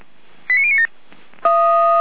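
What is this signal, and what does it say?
Five-tone selective call (ZVEI), heard as quick runs of five tones, tones of about 70 ms. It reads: 89097
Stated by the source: Five-tone ZVEI selective call received on an SDR.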